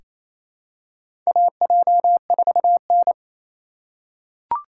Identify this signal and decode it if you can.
Morse code, heard as a keyed tone, about 29 wpm, 690 Hz. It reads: AJ4N
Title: USA Call Signs - 4 Characters - 2x Character Spacing - 28wpm